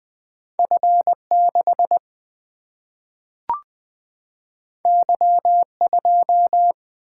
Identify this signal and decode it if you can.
Morse code, heard as a keyed tone, about 20 wpm, 700 Hz. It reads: F 6 Y 2
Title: Single-Letter Single-Number 20wpm